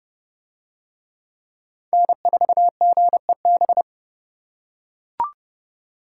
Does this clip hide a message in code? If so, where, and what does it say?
Yes: Morse code from 1.9 to 3.8 s, heard as a keyed tone, about 27 wpm, 700 Hz. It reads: N4GEB